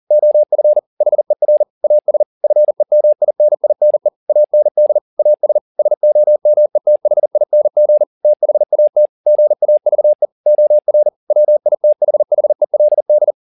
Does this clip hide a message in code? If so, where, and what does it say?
Yes: Morse code from 0.1 to 13.3 s, heard as a keyed tone, about 40 wpm, 600 Hz. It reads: OF HER AS FEMININE AND AS SOMETHING THAT GAVE OR WITHHELD